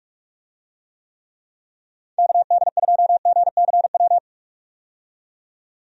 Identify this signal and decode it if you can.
Morse code, heard as a keyed tone, about 46 wpm, 700 Hz. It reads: KD2GCW
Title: USA Call Signs - 6 Characters - 45wpm